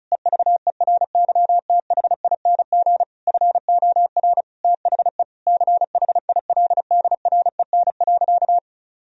Morse code, heard as a keyed tone, about 36 wpm, 700 Hz. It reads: EVERYTHING FOR THE CHILDREN.
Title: Sentences from Top 1000 Words - Review Part 2 - 35wpm